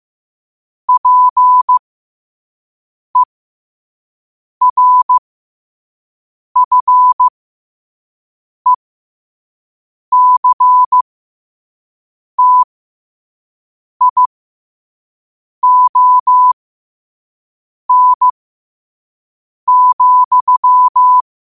Morse code, heard as a keyed tone, about 15 wpm, spaced as for 5 wpm, 1000 Hz. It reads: PERFECTION,